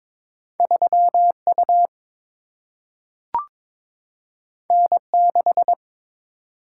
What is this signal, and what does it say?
Morse code, heard as a keyed tone, about 22 wpm, 700 Hz. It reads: 3 U N 6
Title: Single-Letter Single-Number 22wpm